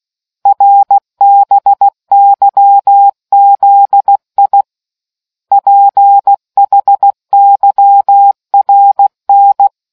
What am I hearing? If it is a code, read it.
Morse code, heard as a keyed tone, about 16 wpm, 800 Hz. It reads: RBYZI PHYRN